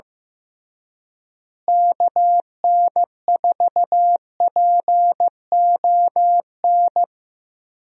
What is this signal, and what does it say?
Morse code, heard as a keyed tone, about 15 wpm, 700 Hz. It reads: KN4PON
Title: USA Call Signs - 6 Characters - Repeated 3x - 15wpm